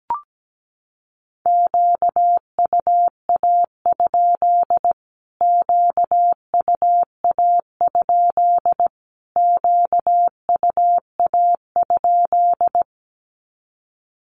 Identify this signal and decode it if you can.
Morse code, heard as a keyed tone, about 17 wpm, 700 Hz. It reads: QUA? QUA? QUA?
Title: Q-Codes Repeated 3x 17wpm